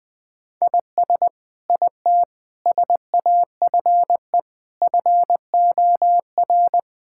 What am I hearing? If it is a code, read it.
Morse code, heard as a keyed tone, about 20 wpm, 700 Hz. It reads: IS IT SAFE FOR
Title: Sentences from Top 1600 Words 20wpm